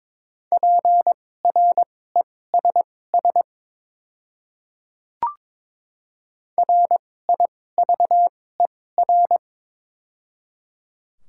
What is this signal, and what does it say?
Morse code, heard as a keyed tone, about 22 wpm, spaced as for 16 wpm, 700 Hz. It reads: PRESS RIVER
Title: 2x Character Spacing - Top 300 Words - 22wpm